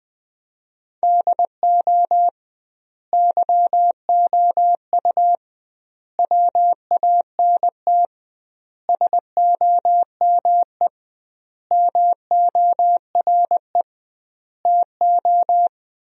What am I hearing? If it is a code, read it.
Morse code, heard as a keyed tone, about 21 wpm, 700 Hz. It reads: DO YOU WANT SOME MORE TO
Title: 2x Word-Spacing - Sentences from Top 300 Words - 20wpm